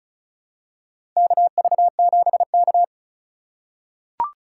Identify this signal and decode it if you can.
Morse code, heard as a keyed tone, about 36 wpm, 700 Hz. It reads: KV7K
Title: USA Call Signs - 4 Characters - 8x Character Spacing - 35wpm